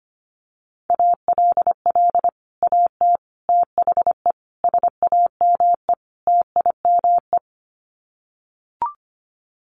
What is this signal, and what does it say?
Morse code, heard as a keyed tone, about 25 wpm, 700 Hz. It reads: ALL AT THE SAME TIME
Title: Sets of 5 Words Encore2 25wpm